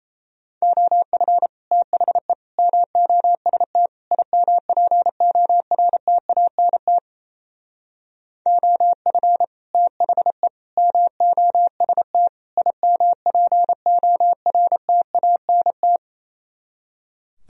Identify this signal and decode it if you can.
Morse code, heard as a keyed tone, about 34 wpm, 700 Hz. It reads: OF THE MOST IMPORTANT OF THE MOST IMPORTANT
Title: Speed-Racing - Sets of 4 Words 28wpm